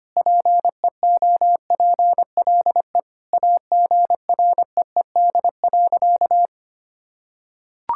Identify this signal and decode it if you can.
Morse code, heard as a keyed tone, about 25 wpm, 700 Hz. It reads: PEOPLE AGREED.
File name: practice / 25wpm / Sentences with 3-Word Phrases - 25wpm